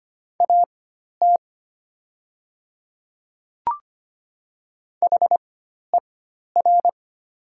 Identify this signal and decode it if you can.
Morse code, heard as a keyed tone, about 25 wpm, spaced as for 12 wpm, 700 Hz. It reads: AT HER
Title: 4x Character Spacing - Top 100 Words - 25wpm